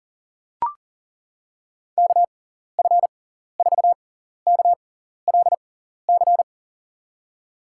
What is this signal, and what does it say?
Morse code, heard as a keyed tone, about 40 wpm, spaced as for 14 wpm, 700 Hz. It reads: KF4KLC
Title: USA Call Signs - 6 Characters - 6x Character Spacing - 40wpm